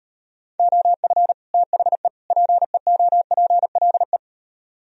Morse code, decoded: OF THE PEOPLE

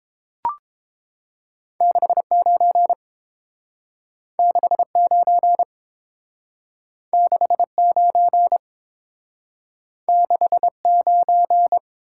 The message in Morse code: 69 69 69 69

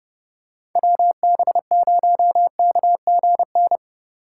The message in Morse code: WB0KGN

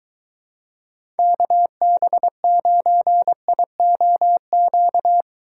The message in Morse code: KB9IOQ

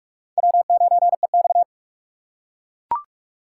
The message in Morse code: W9EX

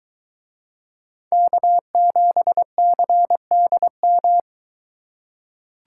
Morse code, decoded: K7CDM

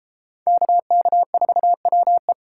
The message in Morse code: KK4WE

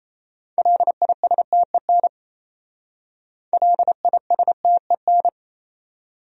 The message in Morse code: LISTEN LISTEN

